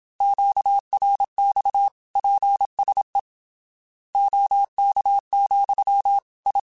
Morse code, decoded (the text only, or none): QRX PSE OK, I